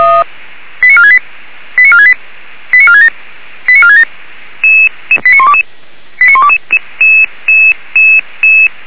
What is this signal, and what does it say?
Five-tone selective call (ZVEI), heard as quick runs of five tones, tones of about 70 ms. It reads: 89468 89468 89467 89467 89122 89122